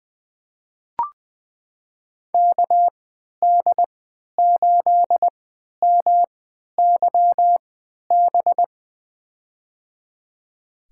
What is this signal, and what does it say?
Morse code, heard as a keyed tone, about 20 wpm, spaced as for 11 wpm, 700 Hz. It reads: KD8MYB
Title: USA Call Signs - 3x Character Spacing - 20wpm